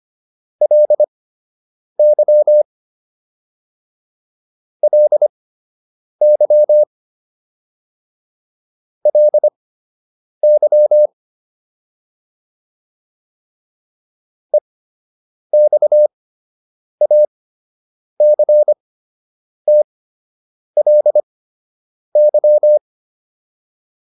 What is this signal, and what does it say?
Morse code, heard as a keyed tone, about 25 wpm, spaced as for 8 wpm, 600 Hz. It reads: LY LY LY EXACTLY